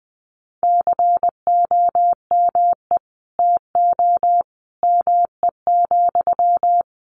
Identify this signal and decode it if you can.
Morse code, heard as a keyed tone, about 19 wpm, 700 Hz. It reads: COME TO ME,
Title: Sentences from Top 1500 Words 20wpm